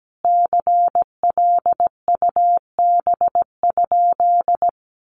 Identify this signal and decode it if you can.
Morse code, heard as a keyed tone, about 17 wpm, 700 Hz. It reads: CLUB?
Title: Sentences from Top 1200 Words 17wpm